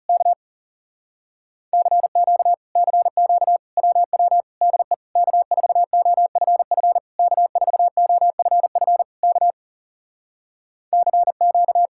Morse code, heard as a keyed tone, about 40 wpm, 700 Hz. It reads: K CQ CQ WW DE K4OFF K4OFF K CQ